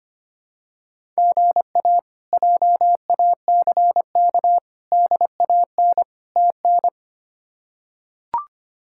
Morse code, heard as a keyed tone, about 25 wpm, 700 Hz. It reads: GA JACK DAN TN
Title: K1USN SST QSO Practice - 25wpm